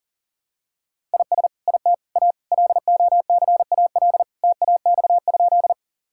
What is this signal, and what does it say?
Morse code, heard as a keyed tone, about 40 wpm, 700 Hz. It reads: IS IT A LOCAL TAX?
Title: Sentences from Top 1000 Words - Review Part 5 - 40wpm